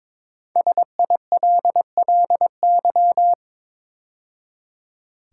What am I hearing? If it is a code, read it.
Morse code, heard as a keyed tone, about 22 wpm, 700 Hz. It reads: SILLY
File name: 5 Letter Words - First 3 Letters Given - 22wpm